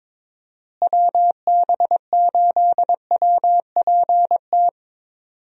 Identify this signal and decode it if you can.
Morse code, heard as a keyed tone, about 22 wpm, 700 Hz. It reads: WB8WPT
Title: USA Call Signs - 6 Characters - Repeated 3x - 22wpm